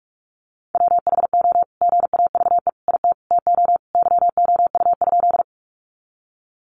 Morse code, heard as a keyed tone, about 44 wpm, 700 Hz. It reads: WHO GAVE IT TO YOU?
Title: Sentences from Top 500 Words 45wpm